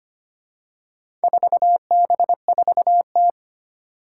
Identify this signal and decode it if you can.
Morse code, heard as a keyed tone, about 25 wpm, 700 Hz. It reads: 4B4T
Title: Four Challenging Letter-Number Combinations 25wpm